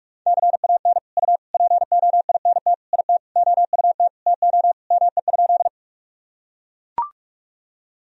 Morse code, heard as a keyed tone, about 42 wpm, 700 Hz. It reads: CAN U POINT IT OUT TO ME?
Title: Sentences from Top 200 Words and Abbreviations 45wpm